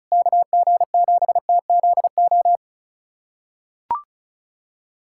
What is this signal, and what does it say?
Morse code, heard as a keyed tone, about 36 wpm, 700 Hz. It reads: KG7TZO
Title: USA Callsign DE Callsign - 6 Characters - 35wpm